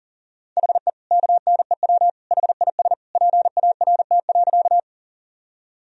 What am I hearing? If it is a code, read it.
Morse code, heard as a keyed tone, about 37 wpm, 700 Hz. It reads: HE KNEW HIS PART.
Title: Sentences from Top 1000 Words - Review Part 1 - 40wpm